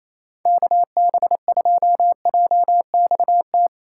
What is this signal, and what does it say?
Morse code, heard as a keyed tone, about 28 wpm, 700 Hz. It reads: KB2JXT
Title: USA Call Signs - 6 Characters - 6x Character Spacing - 28wpm